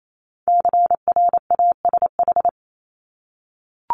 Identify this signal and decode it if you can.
Morse code, heard as a keyed tone, about 28 wpm, 700 Hz. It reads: CRASH
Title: Top 1751-2000 Words 28wpm